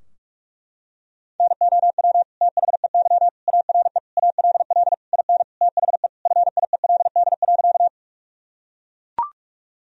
Morse code, heard as a keyed tone, about 41 wpm, 700 Hz. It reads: NOW THEY ARE ALL IN THE FIELD.
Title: Sentences from Top 1000 Words - Review Part 2 - 45wpm